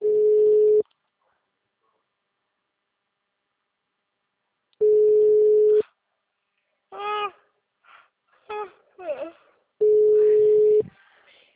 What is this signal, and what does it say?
Ringback tone (Europe, 425 Hz), with speech.